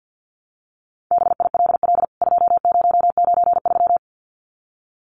Morse code, heard as a keyed tone, about 48 wpm, 700 Hz. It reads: BILL 2093
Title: CWT Exchange 50wpm